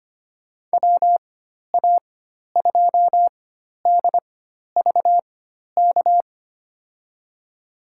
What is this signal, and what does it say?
Morse code, heard as a keyed tone, about 25 wpm, spaced as for 12 wpm, 700 Hz. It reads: WA2DVK
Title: USA Call Signs - 6 Characters - 4x Character Spacing - 25wpm